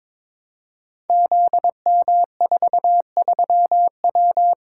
Morse code, heard as a keyed tone, about 22 wpm, 700 Hz. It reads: ZM43W
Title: Code Groups - 5 Letter:Numbers - 22wpm